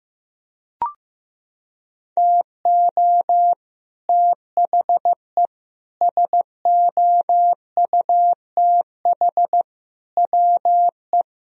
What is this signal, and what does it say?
Morse code, heard as a keyed tone, about 15 wpm, 700 Hz. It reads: TO THE SOUTH WE
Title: Sentences from Top 1000 Words - Review Part 2 - 15wpm